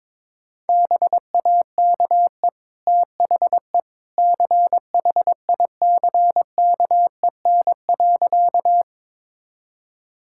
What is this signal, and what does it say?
Morse code, heard as a keyed tone, about 22 wpm, 700 Hz. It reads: BAKE THE CHICKEN.
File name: practice / 22wpm / Easy Sentences with Hints - 22wpm